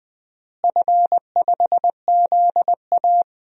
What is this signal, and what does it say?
Morse code, heard as a keyed tone, about 21 wpm, 700 Hz. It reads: F5ZA
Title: DXCC-100 Call Signs - 4 Characters - Repeated 2x - 20wpm